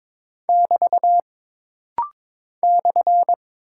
Morse code, heard as a keyed tone, about 22 wpm, 700 Hz. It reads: <BT> /
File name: practice / 22wpm / Single Punctuation-Prosign - Rapid-Fire - 22wpm